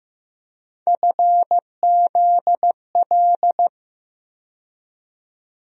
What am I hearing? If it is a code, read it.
Morse code, heard as a keyed tone, about 15 wpm, 700 Hz. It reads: FZL
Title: USA Call Sign Postfixes - 3 Letters 15wpm